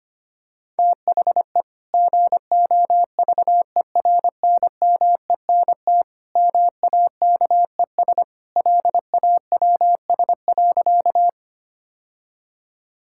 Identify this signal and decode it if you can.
Morse code, heard as a keyed tone, about 25 wpm, 700 Hz. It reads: THE GOVERNMENT MAKES LAWS.
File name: practice / 25wpm / Sentences with 2-Word Phrases - 25wpm